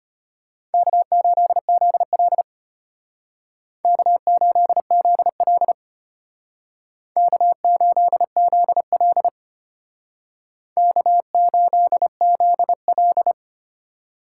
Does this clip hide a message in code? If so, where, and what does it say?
Yes: Morse code from 0.7 to 13.3 s, heard as a keyed tone, about 33 wpm, 700 Hz. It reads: K8ZL K8ZL K8ZL K8ZL